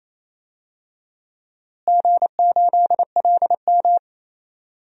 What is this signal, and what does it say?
Morse code, heard as a keyed tone, about 28 wpm, 700 Hz. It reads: G8LM
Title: DXCC-100 Call Signs - 4 Characters - Repeated 3x - 28wpm